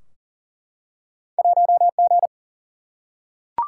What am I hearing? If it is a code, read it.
Morse code, heard as a keyed tone, about 42 wpm, 700 Hz. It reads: 1G
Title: Single-Letter Single-Number 40wpm